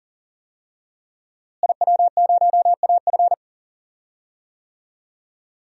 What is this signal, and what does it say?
Morse code, heard as a keyed tone, about 39 wpm, 700 Hz. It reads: IW0AF
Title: DXCC-100 Call Signs - 5 Characters - Repeated 2x - 40wpm